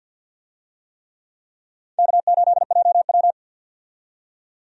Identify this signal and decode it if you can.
Morse code, heard as a keyed tone, about 52 wpm, 700 Hz. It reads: K8JW
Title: USA Call Signs - 4 Characters - 50wpm